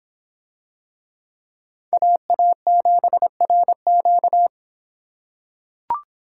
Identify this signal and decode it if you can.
Morse code, heard as a keyed tone, about 25 wpm, 700 Hz. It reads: AA7RQ